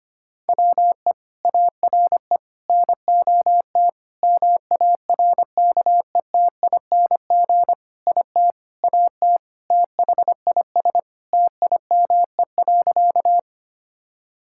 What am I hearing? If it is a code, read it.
Morse code, heard as a keyed tone, about 25 wpm, 700 Hz. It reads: WE ARE NOT MARKETING IT AT THIS TIME.
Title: Sentences from Top 1000 Words - Review Part 5 - 25wpm